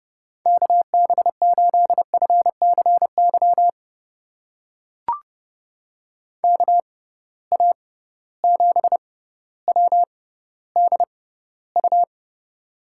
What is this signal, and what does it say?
Morse code, heard as a keyed tone, about 31 wpm, spaced as for 15 wpm, 700 Hz. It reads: KB8FCY KA7WDU
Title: USA Call Signs - 6 Characters - 6x Character Spacing - 30wpm